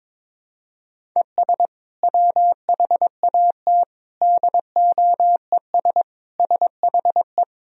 Morse code, heard as a keyed tone, about 22 wpm, 700 Hz. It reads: ES WHAT DOES SHE
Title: Sentences from Top 400 Words and Abbreviations 22wpm